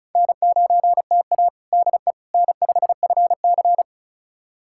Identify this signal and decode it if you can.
Morse code, heard as a keyed tone, about 34 wpm, 700 Hz. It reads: N9TA DE N5FC
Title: USA Callsign DE Callsign - 4 Characters - 35wpm